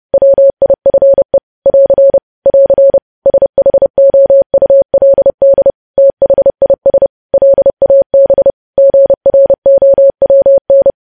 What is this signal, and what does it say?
Morse code, heard as a keyed tone, about 30 wpm, 550 Hz. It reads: WIFE <AR> <AR> SHOULD THIS LAB GROWN
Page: News Headlines in Morse Code at 30 WPM